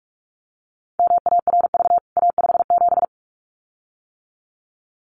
Morse code, heard as a keyed tone, about 44 wpm, 700 Hz. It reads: MARV A57
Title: CWT Exchange 45wpm